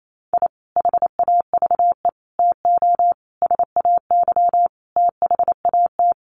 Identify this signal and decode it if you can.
Morse code, heard as a keyed tone, about 28 wpm, 700 Hz. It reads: I HAVE TO SAY THAT
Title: Sets of 5 Words Encore 28wpm